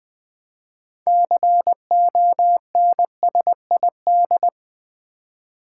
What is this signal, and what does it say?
Morse code, heard as a keyed tone, about 20 wpm, 700 Hz. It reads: CONSID
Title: Build A Word - 6 Letters At A Time - 12-Letter Words - 20wpm